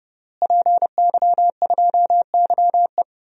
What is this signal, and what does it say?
Morse code, heard as a keyed tone, about 30 wpm, 700 Hz. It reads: PY2YE
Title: DXCC-100 Call Signs - 5 Characters - Repeated 2x - 30wpm